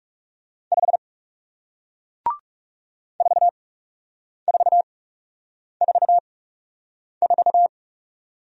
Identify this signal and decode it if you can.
Morse code, heard as a keyed tone, about 37 wpm, spaced as for 8 wpm, 700 Hz. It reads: 5 4444
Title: Speed-Racing Single Letter-Number 30wpm